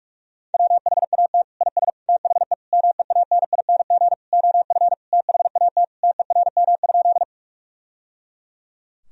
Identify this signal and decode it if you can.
Morse code, heard as a keyed tone, about 46 wpm, 700 Hz. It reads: WHAT IS THE MEANING OF THAT TERM?
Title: Sentences from Top 1000 Words - Review Part 4 - 45wpm